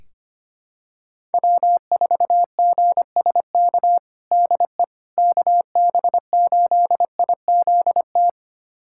Morse code, heard as a keyed tone, about 25 wpm, 700 Hz. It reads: W4GSK DE KB8IZT
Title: USA Callsign DE Callsign 25wpm